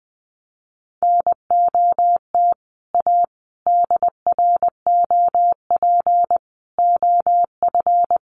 Morse code, heard as a keyed tone, about 20 wpm, 700 Hz. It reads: NOT A DROP OF
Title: Sentences from Top 1400 Words 20wpm